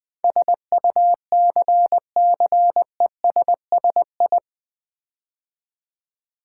Morse code, heard as a keyed tone, about 20 wpm, 700 Hz. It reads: SUCCESSI
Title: Build A Word - 4 Letters At A Time - 10-Letter Words - 20wpm